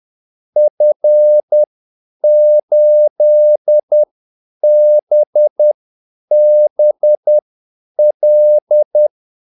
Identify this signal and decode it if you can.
Morse code, heard as a keyed tone, about 10 wpm, spaced as for 8 wpm, 600 Hz. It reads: F8BBL
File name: Callsigns-10-8wpm